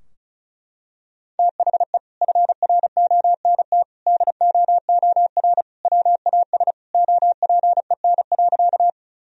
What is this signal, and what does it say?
Morse code, heard as a keyed tone, about 36 wpm, 700 Hz. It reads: THE FRONT DOOR WAS OPEN.
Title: Sentences from Top 1000 Words - Review Part 2 - 35wpm